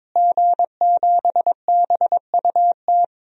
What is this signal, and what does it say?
Morse code, heard as a keyed tone, about 21 wpm, 700 Hz. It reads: G7BUT
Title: DXCC-100 Call Signs - 5 Characters - Repeated 3x - 22wpm